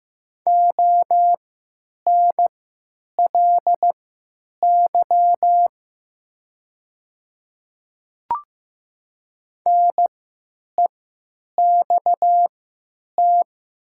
Morse code, heard as a keyed tone, about 15 wpm, spaced as for 9 wpm, 700 Hz. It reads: ONLY NEXT
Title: 3x Character Spacing - Top 300 Words - 15wpm